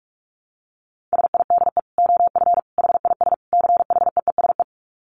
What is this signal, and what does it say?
Morse code, heard as a keyed tone, about 44 wpm, 700 Hz. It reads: SIDE OF HIS CHEESE
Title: Dit Challenge 45wpm